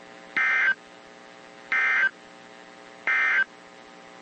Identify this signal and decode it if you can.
Emergency Alert System (SAME) tones, heard as harsh buzzing bursts. It is heard from 0.2 to 3.5 s.
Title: WZ2523 - NOAA Weather Radio